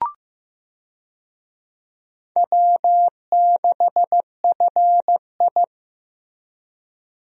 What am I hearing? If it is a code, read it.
Morse code, heard as a keyed tone, about 15 wpm, 700 Hz. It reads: W6FI
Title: USA Call Signs - 4 Characters - Repeated 3x - 15wpm